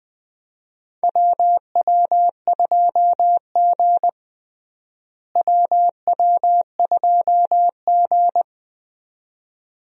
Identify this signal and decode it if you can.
Morse code, heard as a keyed tone, about 20 wpm, 700 Hz. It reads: WW2G WW2G
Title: USA Call Signs - 4 Characters - Repeated 2x - 20wpm